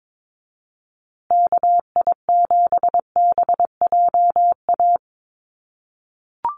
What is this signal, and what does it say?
Morse code, heard as a keyed tone, about 22 wpm, 700 Hz. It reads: KI7BJA
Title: USA Call Signs Repeated 3x 22wpm